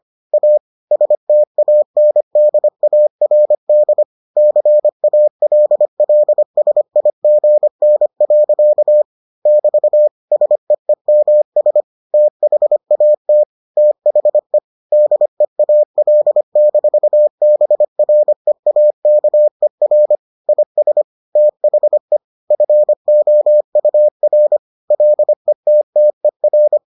Morse code, heard as a keyed tone, about 25 wpm, 600 Hz. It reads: A STANDARD CALLSIGN. <BT> SEEMS THAT THE DEAL-BREAKER IS THE FOUR LETTER